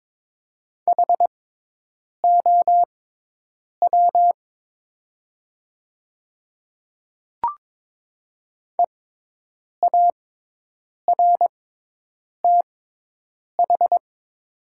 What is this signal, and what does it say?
Morse code, heard as a keyed tone, about 22 wpm, spaced as for 8 wpm, 700 Hz. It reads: HOW EARTH
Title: 6x Character Spacing - Top 200 Words - 22wpm